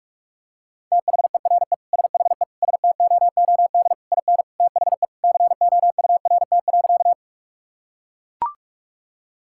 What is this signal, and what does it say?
Morse code, heard as a keyed tone, about 41 wpm, 700 Hz. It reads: THERE SHE STOOD IN THE COURT.
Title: Sentences from Top 1600 Words 45wpm